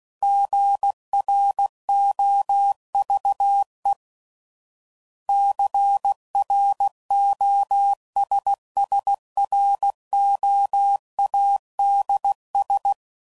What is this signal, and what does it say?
Morse code, heard as a keyed tone, about 16 wpm, 800 Hz. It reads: GROVE CROSSROADS
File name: CW Code Groups @15WPM